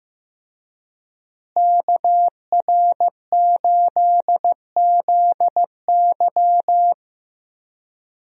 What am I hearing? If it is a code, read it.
Morse code, heard as a keyed tone, about 15 wpm, 700 Hz. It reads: KR8ZY
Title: USA Call Signs - 5 Characters - Repeated 3x - 15wpm